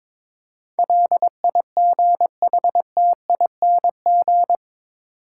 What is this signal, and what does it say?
Morse code, heard as a keyed tone, about 22 wpm, 700 Hz. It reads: LIGHTING
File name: practice / 22wpm / Eight Letter Words - 22wpm